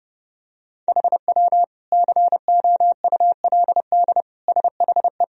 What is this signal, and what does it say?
Morse code, heard as a keyed tone, about 32 wpm, 700 Hz. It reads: HW COULD SHE
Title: Sentences from Top 200 Words and Abbreviations 30wpm